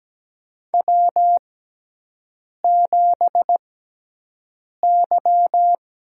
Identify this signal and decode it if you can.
Morse code, heard as a keyed tone, about 17 wpm, spaced as for 6 wpm, 700 Hz. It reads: W7Y
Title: USA Call Signs - 4 Characters - 6x Character Spacing - 17wpm